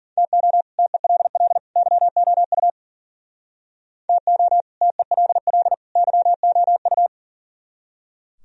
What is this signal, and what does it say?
Morse code, heard as a keyed tone, about 51 wpm, 700 Hz. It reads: TO TELL YOU TO TELL YOU